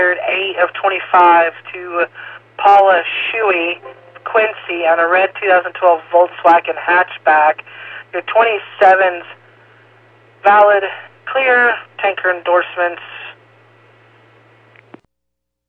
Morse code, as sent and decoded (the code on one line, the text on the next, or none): none